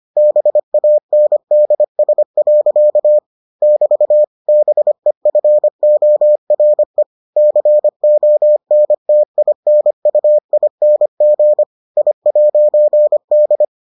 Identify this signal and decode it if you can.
Morse code, heard as a keyed tone, about 25 wpm, 600 Hz. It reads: BANDS. <BT> BEFORE CONTINUING I'D